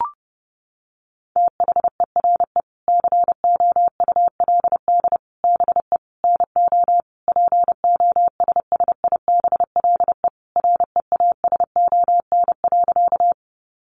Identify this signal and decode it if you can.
Morse code, heard as a keyed tone, about 30 wpm, 700 Hz. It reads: THERE COULD BE NO POSSIBLE REASON.
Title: Sentences from Top 700 Words 30wpm